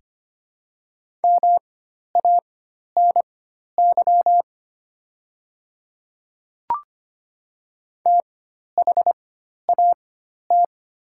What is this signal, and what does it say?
Morse code, heard as a keyed tone, about 25 wpm, spaced as for 12 wpm, 700 Hz. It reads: MANY THAT
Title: 4x Character Spacing - Top 300 Words - 25wpm